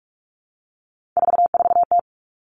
Morse code, heard as a keyed tone, about 46 wpm, 700 Hz. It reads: 44T
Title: Three Challenging Letter-Number Combinations 45wpm